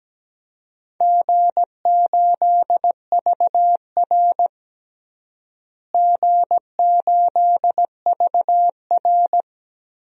Morse code, heard as a keyed tone, about 17 wpm, 700 Hz. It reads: G8VR G8VR
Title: DXCC-100 Call Signs - 4 Characters - Repeated 3x - 17wpm